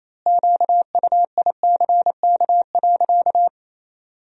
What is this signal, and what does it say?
Morse code, heard as a keyed tone, about 28 wpm, 700 Hz. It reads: QUICK.
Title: 4x Word-Spacing - Sentences from Top 500 Words - 28wpm